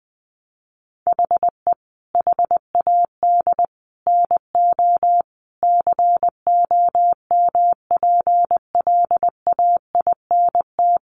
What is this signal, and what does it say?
Morse code, heard as a keyed tone, about 20 wpm, 700 Hz. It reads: HE HAD NO COMPLAINT
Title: Sentences from Top 1400 Words 20wpm